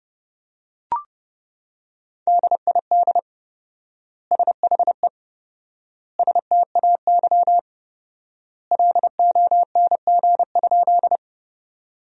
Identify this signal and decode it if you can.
Morse code, heard as a keyed tone, about 32 wpm, 700 Hz. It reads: DID SHE STAY LONG?
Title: 4x Word-Spacing - Sentences from Top 500 Words - 30wpm